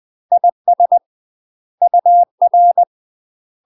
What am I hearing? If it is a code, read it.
Morse code, heard as a keyed tone, about 20 wpm, 700 Hz. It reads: IS UR
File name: UK MASTER QSOs TRAINING / 20WPM / UK MASTER QSOs 0013